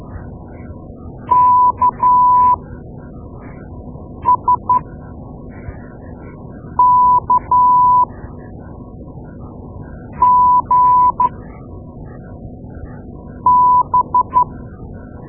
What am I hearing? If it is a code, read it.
Morse code, heard as a keyed tone, about 12 wpm, spaced as for 4 wpm, 990 Hz. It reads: KSKGB
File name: Morse Lessons / Morse Lessons / Tape-2A